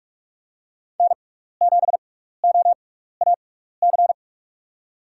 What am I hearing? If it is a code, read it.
Morse code, heard as a keyed tone, about 47 wpm, spaced as for 16 wpm, 700 Hz. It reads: N7OAC
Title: USA Call Signs - 5 Characters - 6x Character Spacing - 45wpm